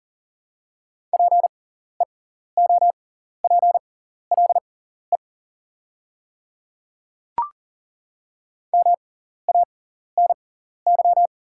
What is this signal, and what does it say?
Morse code, heard as a keyed tone, about 41 wpm, spaced as for 14 wpm, 690 Hz. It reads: PEOPLE MANY
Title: 6x Character Spacing - Top 100 Words - 40wpm